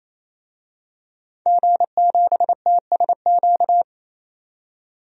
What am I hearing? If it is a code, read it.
Morse code, heard as a keyed tone, about 28 wpm, 700 Hz. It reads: G7TSQ
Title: DXCC-100 Call Signs - 5 Characters - 28wpm